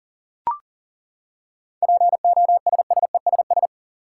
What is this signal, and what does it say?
Morse code, heard as a keyed tone, about 37 wpm, 700 Hz. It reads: POSSESS
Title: Seven Letter Words - 40wpm